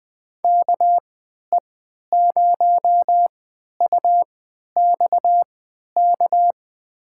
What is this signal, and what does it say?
Morse code, heard as a keyed tone, about 21 wpm, spaced as for 11 wpm, 700 Hz. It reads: KE0UXK
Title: USA Call Signs - 6 Characters - 3x Character Spacing - 20wpm